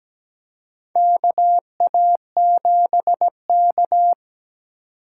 Morse code, heard as a keyed tone, about 17 wpm, 700 Hz. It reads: KA7K
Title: USA Call Signs - 4 Characters - 3x Character Spacing - 17wpm